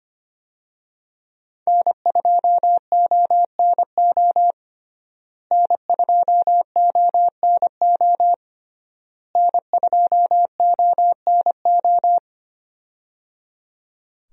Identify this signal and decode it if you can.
Morse code, heard as a keyed tone, about 25 wpm, 700 Hz. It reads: N2ONO N2ONO N2ONO
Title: USA Call Signs - 5 Characters - Repeated 3x - 25wpm